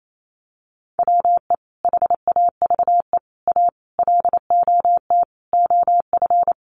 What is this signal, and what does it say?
Morse code, heard as a keyed tone, about 28 wpm, 700 Hz. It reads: WE HAVE A LOT OF
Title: Sets of 5 Words 28wpm